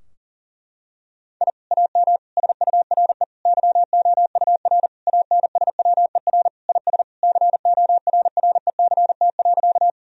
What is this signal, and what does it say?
Morse code, heard as a keyed tone, about 37 wpm, 700 Hz. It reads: I AM SURE YOUR ANSWER IS CORRECT.
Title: Sentences from Top 1000 Words - Review Part 2 - 40wpm